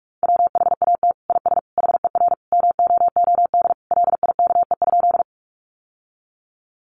Morse code, heard as a keyed tone, about 44 wpm, 700 Hz. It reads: WHAT IS HER MOOD LIKE?
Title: Sentences from Top 1100 Words 45wpm